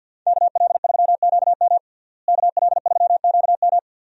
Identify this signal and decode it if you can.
Morse code, heard as a keyed tone, about 47 wpm, 700 Hz. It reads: KL3QM KL3QM
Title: USA Call Signs - 5 Characters - Repeated 2x - 50wpm